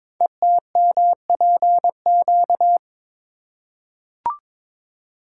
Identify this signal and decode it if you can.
Morse code, heard as a keyed tone, about 22 wpm, 700 Hz. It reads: ETMPQ
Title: Code Groups - 5 Letters - 22wpm